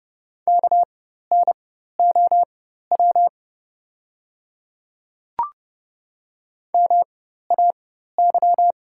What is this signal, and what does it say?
Morse code, heard as a keyed tone, about 30 wpm, spaced as for 14 wpm, 690 Hz. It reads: KNOW MAY
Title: 4x Character Spacing - Top 100 Words - 30wpm